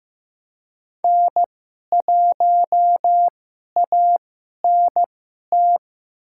Morse code, heard as a keyed tone, about 16 wpm, spaced as for 11 wpm, 700 Hz. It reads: N1ANT
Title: USA Call Signs - 5 Characters - 2x Character Spacing - 15wpm